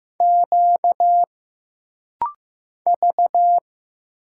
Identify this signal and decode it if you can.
Morse code, heard as a keyed tone, about 15 wpm, 700 Hz. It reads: Q V
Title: Single Letters - Rapid-Fire 15wpm